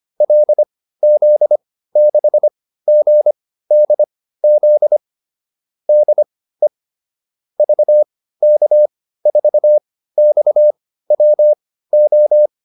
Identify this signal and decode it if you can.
Morse code, heard as a keyed tone, about 25 wpm, spaced as for 15 wpm, 600 Hz. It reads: LZ6GDZ DE VK4XWO